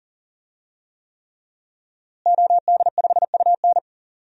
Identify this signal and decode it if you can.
Morse code, heard as a keyed tone, about 40 wpm, 700 Hz. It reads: OD5UN
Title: DXCC-100 Call Signs - 5 Characters - Repeated 3x - 40wpm